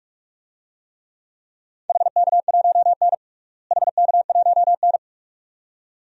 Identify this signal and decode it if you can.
Morse code, heard as a keyed tone, about 46 wpm, 700 Hz. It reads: HK1N HK1N
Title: DXCC-100 Call Signs - 4 Characters - Repeated 2x - 45wpm